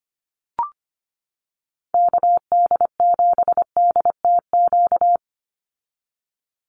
Morse code, heard as a keyed tone, about 25 wpm, 700 Hz. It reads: KD7DTQ
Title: USA Call Signs 25wpm